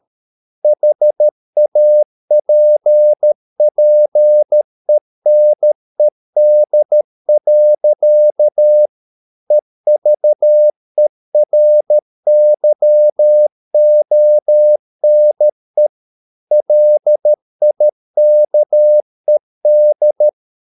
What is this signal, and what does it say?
Morse code, heard as a keyed tone, about 13 wpm, 600 Hz. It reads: HAPPENED. EVERYONE LIKED